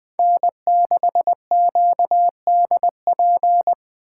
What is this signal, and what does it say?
Morse code, heard as a keyed tone, about 24 wpm, 700 Hz. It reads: N6QDP
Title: Speed Racing - USA 1x3 Call Signs 20wpm